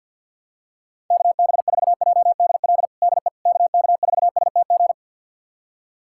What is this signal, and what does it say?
Morse code, heard as a keyed tone, about 52 wpm, 700 Hz. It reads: KB4JDL DE KK4STG
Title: USA Callsign DE Callsign 50wpm